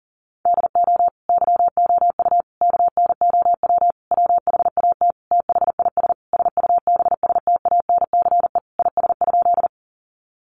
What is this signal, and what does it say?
Morse code, heard as a keyed tone, about 40 wpm, 700 Hz. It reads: DO YOU KNOW WHAT THIS SUBSTANCE IS?
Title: Sentences from Top 1500 Words 40wpm